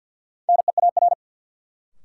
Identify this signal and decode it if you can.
Morse code, heard as a keyed tone, about 53 wpm, 700 Hz. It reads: NEAR
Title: Sentences from Top 200 Words - New Word Drill - 50wpm